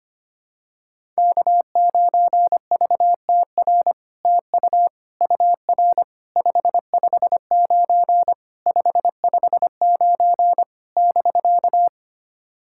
Morse code, heard as a keyed tone, about 25 wpm, 700 Hz. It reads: K9VTR TU UR 559 559 <BK>
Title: SOTA QSO Practice 25wpm